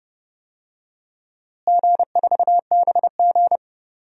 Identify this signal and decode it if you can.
Morse code, heard as a keyed tone, about 30 wpm, 700 Hz. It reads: G4BG
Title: DXCC-100 Call Signs - 4 Characters - Repeated 2x - 30wpm